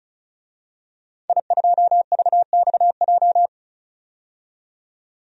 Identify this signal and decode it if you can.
Morse code, heard as a keyed tone, about 34 wpm, 700 Hz. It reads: I2VXJ
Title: DXCC-100 Call Signs - 5 Characters - 35wpm